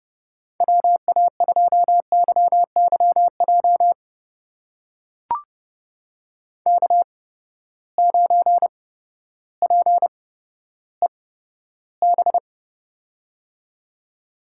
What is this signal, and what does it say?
Morse code, heard as a keyed tone, about 30 wpm, spaced as for 14 wpm, 700 Hz. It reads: WA2YYJ K9PEB
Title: USA Call Signs - 8x Character Spacing - 30wpm